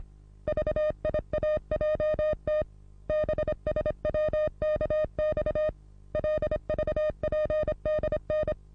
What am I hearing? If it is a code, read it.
Morse code, heard as a keyed tone, about 28 wpm, 610 Hz. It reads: VIAJT BSWKX LVPDN
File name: repos / Morsekurs / DJ4UF Morsen27